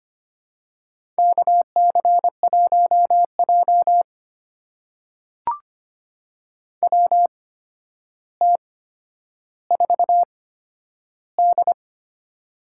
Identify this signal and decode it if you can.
Morse code, heard as a keyed tone, about 25 wpm, spaced as for 11 wpm, 700 Hz. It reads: KC1J WT4D
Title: USA Call Signs - 4 Characters - 8x Character Spacing - 25wpm